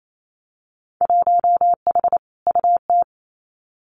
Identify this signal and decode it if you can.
Morse code, heard as a keyed tone, about 28 wpm, 700 Hz. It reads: 1H UT
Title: Winter Field Day Exchanges 28wpm